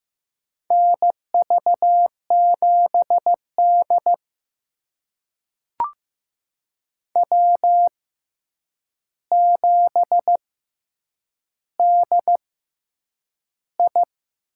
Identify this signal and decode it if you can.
Morse code, heard as a keyed tone, about 15 wpm, spaced as for 8 wpm, 700 Hz. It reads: NV7D W7DI
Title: USA Call Signs - 4 Characters - 6x Character Spacing - 15wpm